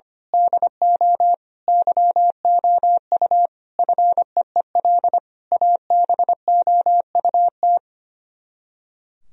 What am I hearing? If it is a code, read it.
Morse code, heard as a keyed tone, about 24 wpm, 700 Hz. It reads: DO YOU FEEL ABOUT
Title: Speed-Racing - Sets of 5 Words 25wpm